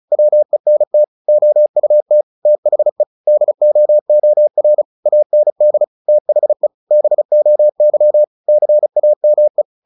Morse code, heard as a keyed tone, about 34 wpm, 600 Hz. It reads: WENT OUT THE DOOR AND THE BOY CAME
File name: the old man and the sea / the old man and the sea 1 35wpm